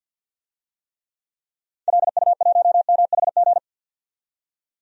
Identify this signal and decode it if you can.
Morse code, heard as a keyed tone, about 51 wpm, 700 Hz. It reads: LU1MHG
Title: DXCC-100 Call Signs - 6 Characters - 50wpm